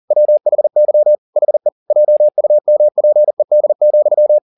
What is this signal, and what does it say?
Morse code, heard as a keyed tone, about 40 wpm, 600 Hz. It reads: WHY HE JUMPED,